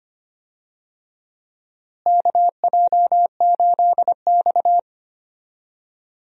Morse code, heard as a keyed tone, about 25 wpm, 700 Hz. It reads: KJ8X
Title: USA Call Signs - 4 Characters - Repeated 3x - 25wpm